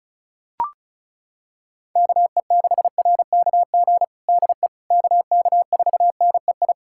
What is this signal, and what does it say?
Morse code, heard as a keyed tone, about 36 wpm, 700 Hz. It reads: KE6RKG DE KK4NEI
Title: USA Callsign DE Callsign 35wpm